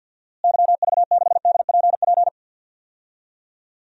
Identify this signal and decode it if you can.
Morse code, heard as a keyed tone, about 48 wpm, 700 Hz. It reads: YV6DPP
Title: DXCC-100 Call Signs - 6 Characters - Repeated 2x - 50wpm